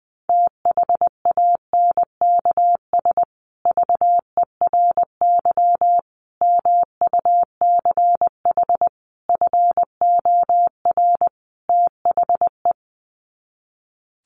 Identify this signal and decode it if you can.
Morse code, heard as a keyed tone, about 20 wpm, 700 Hz. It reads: THANKS VERY MUCH FOR THE
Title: Sets of 5 Words Encore2 20wpm